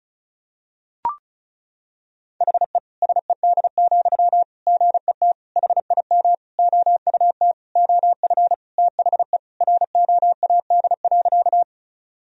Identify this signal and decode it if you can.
Morse code, heard as a keyed tone, about 34 wpm, 700 Hz. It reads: HE SED, GET HIM OUT OF THE ROAD.
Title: Sentences from Top 400 Words and Abbreviations 35wpm